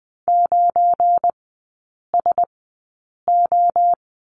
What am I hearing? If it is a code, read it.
Morse code, heard as a keyed tone, about 20 wpm, spaced as for 8 wpm, 700 Hz. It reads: 9SO
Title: 178 Lesson 29e - Slash - Three Characters 20wpm